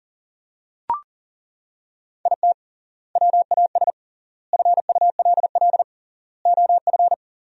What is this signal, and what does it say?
Morse code, heard as a keyed tone, about 40 wpm, 700 Hz. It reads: IT WAS FULL OF
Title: 3x Word-Spacing - Sentences from Top 500 Words - 40wpm